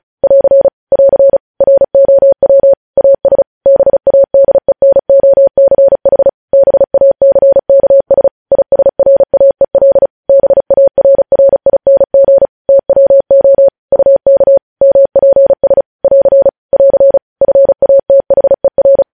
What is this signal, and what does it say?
Morse code, heard as a keyed tone, about 36 wpm, 550 Hz. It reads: <AR> <AR> ROW AS BADENOCH BACKS ISRAEL BARRING TWO UK MPS <AR> <AR> FATHER
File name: podcast / News headlines-1x-35x35